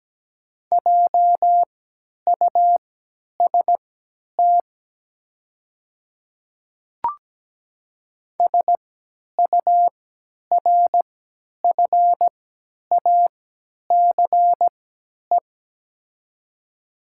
Morse code, heard as a keyed tone, about 17 wpm, spaced as for 10 wpm, 690 Hz. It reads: JUST SURFACE
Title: 3x Character Spacing - Top 500 Words - 17wpm